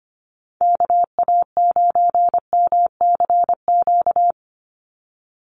Morse code, heard as a keyed tone, about 25 wpm, 700 Hz. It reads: KA9MCQ
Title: USA Call Signs 25wpm